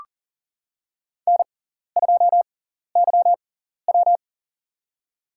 Morse code, heard as a keyed tone, about 40 wpm, spaced as for 14 wpm, 700 Hz. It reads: N2YW
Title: USA Call Signs - 4 Characters - 6x Character Spacing - 40wpm